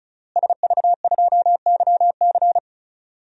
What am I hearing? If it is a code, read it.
Morse code, heard as a keyed tone, about 36 wpm, 700 Hz. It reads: SV2YC
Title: DXCC-100 Call Signs - 5 Characters - Repeated 3x - 35wpm